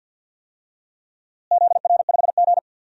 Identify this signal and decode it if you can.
Morse code, heard as a keyed tone, about 53 wpm, 700 Hz. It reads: ZR5G
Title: DXCC-100 Call Signs - 4 Characters - Repeated 2x - 50wpm